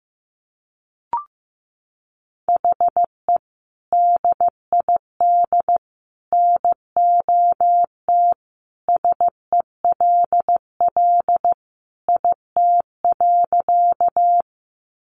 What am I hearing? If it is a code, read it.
Morse code, heard as a keyed tone, about 15 wpm, 700 Hz. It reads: HE DID NOT SELL IT.
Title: Sentences from Top 1100 Words 15wpm